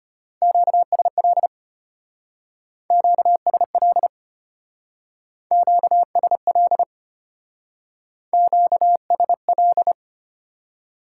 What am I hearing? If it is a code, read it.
Morse code, heard as a keyed tone, about 30 wpm, 700 Hz. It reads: QSL QSL QSL QSL
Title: Speed-Racing Top 90 QSO Elements 25wpm